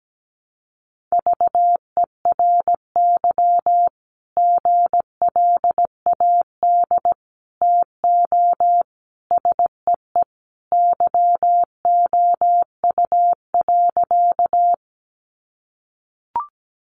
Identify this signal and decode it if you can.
Morse code, heard as a keyed tone, about 17 wpm, 700 Hz. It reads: VERY GLAD TO SEE YOU.
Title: Sentences from Top 800 Words 17wpm